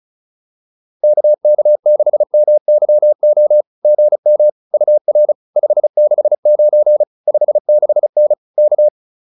Morse code, heard as a keyed tone, about 34 wpm, 600 Hz. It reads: KK6MYO GM UR 569 56N K